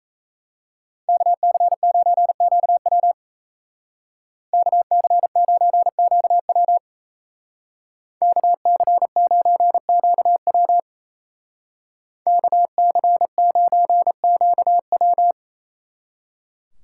Morse code, heard as a keyed tone, about 33 wpm, 700 Hz. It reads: KC9QW KC9QW KC9QW KC9QW
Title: Speed Racing - USA 2x2 Call Signs 28wpm